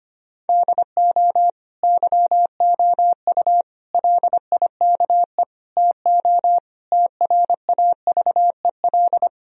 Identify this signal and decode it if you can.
Morse code, heard as a keyed tone, about 25 wpm, 700 Hz. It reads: DO YOU LIKE TO TRAVEL
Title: Sentences from Top 1000 Words - Review Part 3 - 25wpm